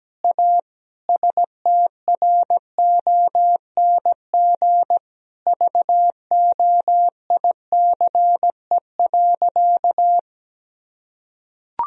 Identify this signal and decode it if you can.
Morse code, heard as a keyed tone, about 17 wpm, 700 Hz. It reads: A STRONG VOICE.
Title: Sentences from Top 1000 Words - Review Part 3 - 17wpm